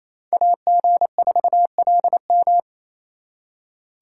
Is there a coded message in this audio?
Yes: Morse code from 0.3 to 2.6 s, heard as a keyed tone, about 36 wpm, 700 Hz. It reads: AG4LM